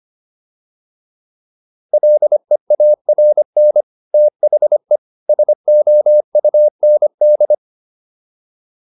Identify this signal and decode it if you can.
Morse code, heard as a keyed tone, about 25 wpm, 600 Hz. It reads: LEARN THE SOUND